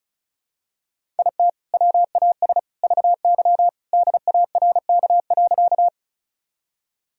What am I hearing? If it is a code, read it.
Morse code, heard as a keyed tone, about 34 wpm, 700 Hz. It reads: IT WAS VY DARK.
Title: Sentences from Top 500 Words and Abbreviations 35wpm